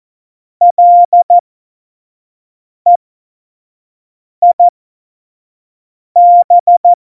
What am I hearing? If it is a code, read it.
Morse code, heard as a keyed tone, about 14 wpm, spaced as for 5 wpm, 700 Hz. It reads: LEIB
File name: Morsekurs / Witze / Witz 05. Januar 2021, 25 F70